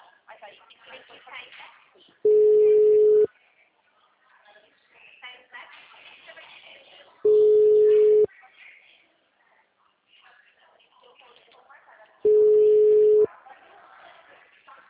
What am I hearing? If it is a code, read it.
Ringback tone (Europe, 425 Hz), with speech.
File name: out- -2005- - - .3723